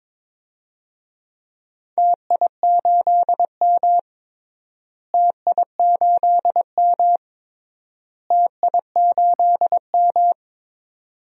Morse code, heard as a keyed tone, about 22 wpm, 700 Hz. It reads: TI8M TI8M TI8M
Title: DXCC-100 Call Signs - 4 Characters - Repeated 3x - 22wpm